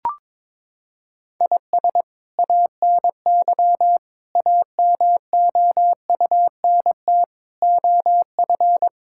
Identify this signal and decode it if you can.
Morse code, heard as a keyed tone, about 22 wpm, 700 Hz. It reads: IS ANY AMOUNT OF
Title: Sentences from Top 1000 Words - Review Part 6 - 22wpm